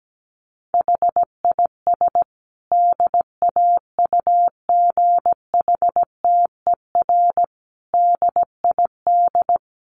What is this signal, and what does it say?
Morse code, heard as a keyed tone, about 17 wpm, 700 Hz. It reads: HIS DAUGHTER DID
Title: Sentences from Top 1500 Words 17wpm